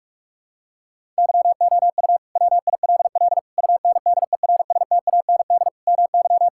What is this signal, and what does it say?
Morse code, heard as a keyed tone, about 48 wpm, 700 Hz. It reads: YOU WILL UNDERSTAND MY
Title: Sentences from Top 1600 Words 45wpm